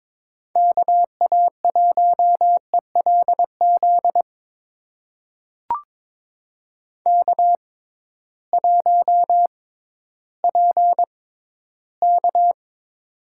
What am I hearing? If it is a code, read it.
Morse code, heard as a keyed tone, about 22 wpm, spaced as for 13 wpm, 700 Hz. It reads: KA1ELZ K1PK
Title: USA Call Signs - 6x Character Spacing - 22wpm